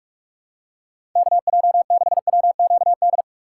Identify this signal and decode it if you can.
Morse code, heard as a keyed tone, about 43 wpm, 700 Hz. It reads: KJ6WQD
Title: USA Call Signs - 6 Characters - 2x Character Spacing - 45wpm